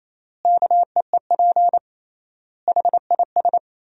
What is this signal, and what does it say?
Morse code, heard as a keyed tone, about 28 wpm, 700 Hz. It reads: KEEP HIS
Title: 3x Word-Spacing - Sentences from Top 300 Words - 28wpm